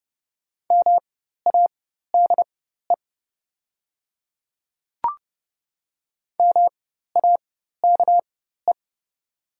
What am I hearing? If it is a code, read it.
Morse code, heard as a keyed tone, about 32 wpm, spaced as for 14 wpm, 700 Hz. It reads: MADE MAKE
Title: 4x Character Spacing - Top 200 Words - 30wpm